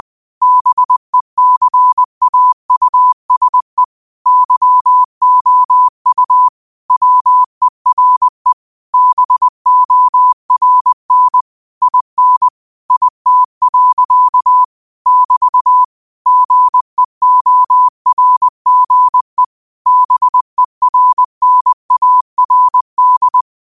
Morse code, heard as a keyed tone, about 20 wpm, 1000 Hz. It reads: BECAUSE YOU WERE BORN IN IT. <BT> GEORGE BERNARD